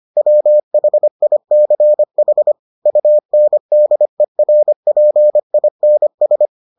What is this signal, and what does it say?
Morse code, heard as a keyed tone, about 25 wpm, 600 Hz. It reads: WHICH UNDERPINS